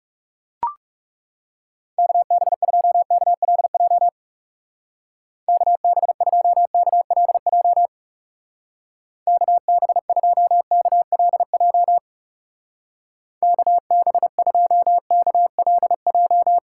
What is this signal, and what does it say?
Morse code, heard as a keyed tone, about 35 wpm, 700 Hz. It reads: KB2KLJ KB2KLJ KB2KLJ KB2KLJ
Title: Speed Racing - USA 2x3 Call Signs 30wpm